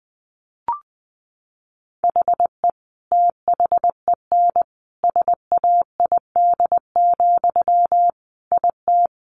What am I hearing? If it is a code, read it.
Morse code, heard as a keyed tone, about 20 wpm, 700 Hz. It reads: HE THEN SAID, IT
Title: Sentences from Top 300 Words 20wpm